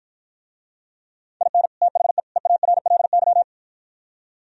Morse code, heard as a keyed tone, about 52 wpm, 700 Hz. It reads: IN THE EARLY